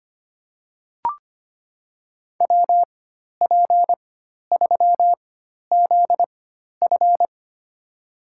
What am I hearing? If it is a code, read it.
Morse code, heard as a keyed tone, about 25 wpm, spaced as for 12 wpm, 700 Hz. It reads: WP3ZF
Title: USA Call Signs - 5 Characters - 4x Character Spacing - 25wpm